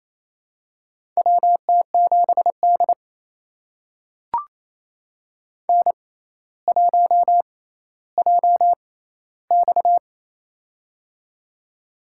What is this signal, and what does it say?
Morse code, heard as a keyed tone, about 29 wpm, spaced as for 15 wpm, 700 Hz. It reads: WT7D N1JX
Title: USA Call Signs - 4 Characters - 6x Character Spacing - 28wpm